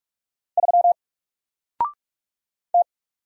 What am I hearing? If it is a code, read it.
Morse code, heard as a keyed tone, about 46 wpm, 700 Hz. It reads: 3 T
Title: Single Character (Letter:Number:Punctuation:Prosign) Alt 45wpm